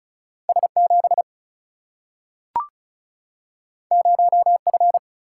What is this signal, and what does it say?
Morse code, heard as a keyed tone, about 36 wpm, 700 Hz. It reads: S7 0F
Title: Single-Letter Single-Number 35wpm